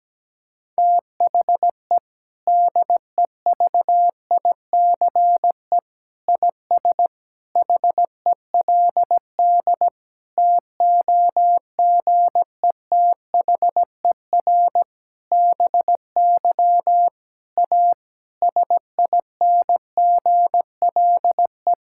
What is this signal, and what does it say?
Morse code, heard as a keyed tone, about 17 wpm, 700 Hz. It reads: THE DEVICE IS HELD TOGETHER BY A SINGLE
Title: Sentences from Top 1600 Words 17wpm